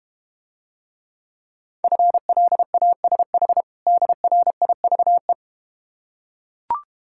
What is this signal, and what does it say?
Morse code, heard as a keyed tone, about 32 wpm, 700 Hz. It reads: FLASH DRIVE